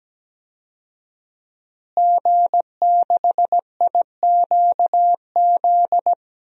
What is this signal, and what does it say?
Morse code, heard as a keyed tone, about 17 wpm, 700 Hz. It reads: G6IQZ
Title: DXCC-100 Call Signs - 5 Characters - Repeated 2x - 17wpm